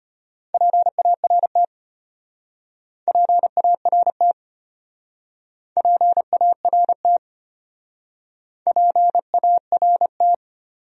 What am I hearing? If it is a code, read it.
Morse code, heard as a keyed tone, about 32 wpm, 700 Hz. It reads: PART PART PART PART